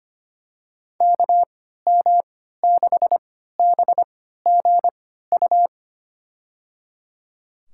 Morse code, heard as a keyed tone, about 25 wpm, spaced as for 14 wpm, 700 Hz. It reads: KM6BGU
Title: USA Call Signs - 6 Characters - 3x Character Spacing - 25wpm